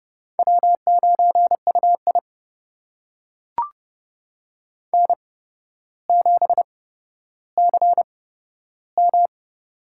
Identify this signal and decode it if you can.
Morse code, heard as a keyed tone, about 30 wpm, spaced as for 13 wpm, 700 Hz. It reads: W9UI N7CM
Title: USA Call Signs - 4 Characters - 8x Character Spacing - 30wpm